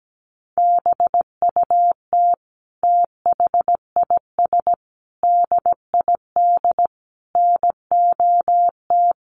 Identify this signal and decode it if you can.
Morse code, heard as a keyed tone, about 17 wpm, 700 Hz. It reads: BUT THIS DID NOT
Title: Sentences from Top 1200 Words 17wpm